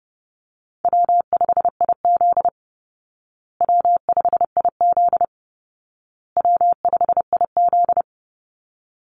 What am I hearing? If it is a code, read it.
Morse code, heard as a keyed tone, about 30 wpm, 700 Hz. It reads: W5IZ W5IZ W5IZ
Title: Easy Call Signs Repeated 3x - 30wpm